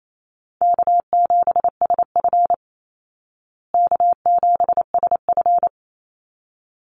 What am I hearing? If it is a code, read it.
Morse code, heard as a keyed tone, about 28 wpm, 700 Hz. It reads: K7SF K7SF
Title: Easy Call Signs Repeated 2x - 28wpm